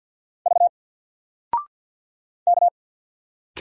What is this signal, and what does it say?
Morse code, heard as a keyed tone, about 48 wpm, 700 Hz. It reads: V K
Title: Single Letter-Number Alt 50wpm